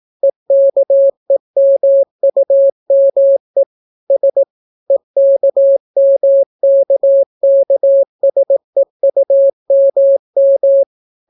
Morse code, heard as a keyed tone, about 18 wpm, 550 Hz. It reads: EKEMUME S EKMKKSEUMM